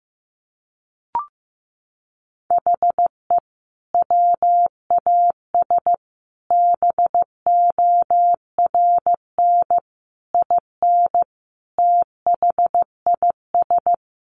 Morse code, heard as a keyed tone, about 15 wpm, 700 Hz. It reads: HE WAS BORN IN THIS
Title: Sentences from Top 1000 Words 15wpm